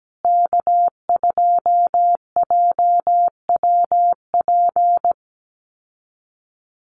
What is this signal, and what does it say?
Morse code, heard as a keyed tone, about 17 wpm, 700 Hz. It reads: K2JWP
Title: USA Call Signs Repeated 3x 17wpm